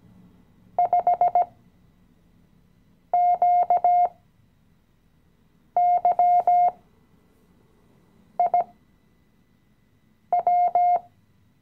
Morse code, heard as a keyed tone, about 16 wpm, spaced as for 5 wpm, 710 Hz. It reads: 5QYIW
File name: CW-Online-Kurs / Anfänger / Lektion 018